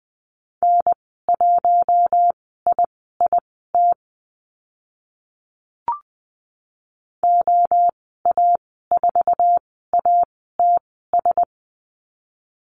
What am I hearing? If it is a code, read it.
Morse code, heard as a keyed tone, about 20 wpm, spaced as for 14 wpm, 700 Hz. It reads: N1IIT OA4ATS
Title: 076 Lesson 13g - Letter L - Call Signs 20wpm